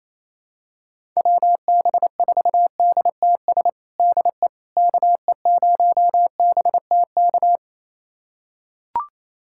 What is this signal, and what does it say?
Morse code, heard as a keyed tone, about 28 wpm, 700 Hz. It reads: WB4DTS DE KE0BTK
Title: USA Callsign DE Callsign - 6 Characters - 28wpm